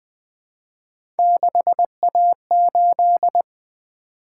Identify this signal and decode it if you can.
Morse code, heard as a keyed tone, about 20 wpm, 700 Hz. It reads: 6A8
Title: Three Letter-Number Combinations - 20wpm